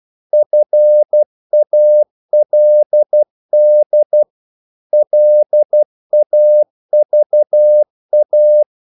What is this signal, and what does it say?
Morse code, heard as a keyed tone, about 12 wpm, 600 Hz. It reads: FALD LAVA
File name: Korte ord DK 12wpm